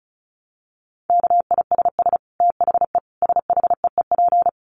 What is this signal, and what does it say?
Morse code, heard as a keyed tone, about 34 wpm, 700 Hz. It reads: KISS THE SHEEP
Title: Dit Challenge 35wpm